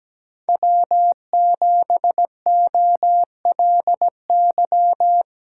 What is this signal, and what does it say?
Morse code, heard as a keyed tone, about 17 wpm, 700 Hz. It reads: W7OLY
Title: USA Call Signs - 3x Character Spacing - 17wpm